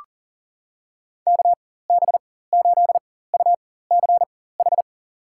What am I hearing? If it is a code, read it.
Morse code, heard as a keyed tone, about 40 wpm, spaced as for 19 wpm, 700 Hz. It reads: KB8UCH
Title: USA Call Signs - 4x Character Spacing - 40wpm